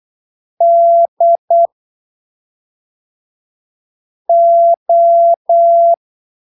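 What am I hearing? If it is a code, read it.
Morse code, heard as a keyed tone, about 8 wpm, spaced as for 3 wpm, 680 Hz. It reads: DO
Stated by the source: Morse training text at 8 ppm for Brazilian class B amateur radio telegraphy exams, from Portal do CW, PY2BRL.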